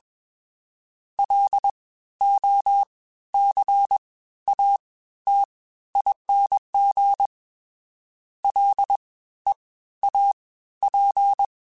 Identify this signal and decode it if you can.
Morse code, heard as a keyed tone, about 21 wpm, spaced as for 13 wpm, 800 Hz. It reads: LOCATING LEAP